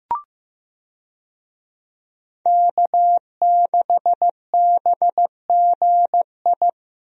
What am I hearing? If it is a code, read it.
Morse code, heard as a keyed tone, about 15 wpm, 700 Hz. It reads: K6BGI
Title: USA Call Signs - 5 Characters - Repeated 2x - 15wpm